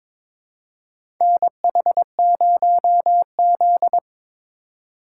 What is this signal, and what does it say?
Morse code, heard as a keyed tone, about 22 wpm, 700 Hz. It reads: NH0Z
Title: USA Call Signs - 4 Characters - 4x Character Spacing - 22wpm